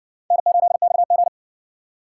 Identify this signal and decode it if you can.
Morse code, heard as a keyed tone, about 55 wpm, 700 Hz. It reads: N8XG